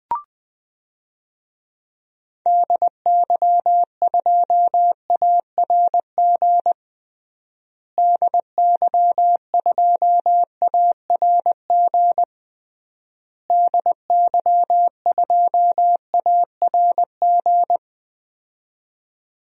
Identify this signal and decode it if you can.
Morse code, heard as a keyed tone, about 20 wpm, 700 Hz. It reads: DY2ARG DY2ARG DY2ARG
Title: DXCC-100 Call Signs - 6 Characters - Repeated 3x - 20wpm